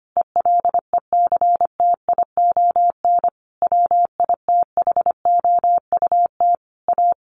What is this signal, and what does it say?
Morse code, heard as a keyed tone, about 25 wpm, 700 Hz. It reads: ELECTION WITHOUT A
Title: Sentences from Top 1500 Words 25wpm